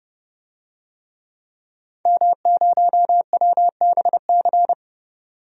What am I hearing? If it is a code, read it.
Morse code, heard as a keyed tone, about 30 wpm, 700 Hz. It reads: M0WBC
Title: DXCC-100 Call Signs - 5 Characters - 30wpm